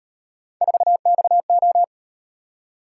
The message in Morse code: 4XO

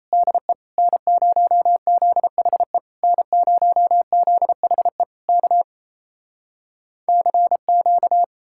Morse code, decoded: DE N0ZHE N0ZHE K CQ

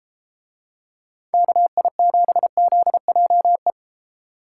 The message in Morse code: KI7ZJE

